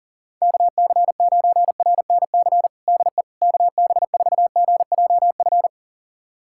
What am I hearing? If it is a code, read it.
Morse code, heard as a keyed tone, about 37 wpm, 700 Hz. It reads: KC9RNC DE KB4GJF
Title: USA Callsign DE Callsign 40wpm